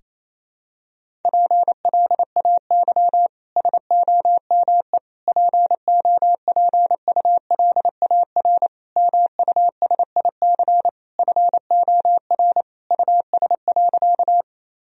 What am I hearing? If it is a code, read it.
Morse code, heard as a keyed tone, about 28 wpm, 700 Hz. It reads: PLAY SOME POPULAR MUSIC FOR US.
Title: Sentences from Top 1000 Words - Review Part 3 - 28wpm